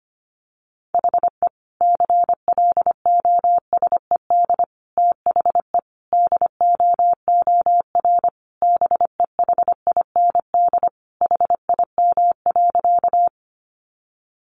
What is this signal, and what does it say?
Morse code, heard as a keyed tone, about 25 wpm, 700 Hz. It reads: HE CLOSED THE DOOR BEHIND HIM.
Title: Sentences from Top 1400 Words 25wpm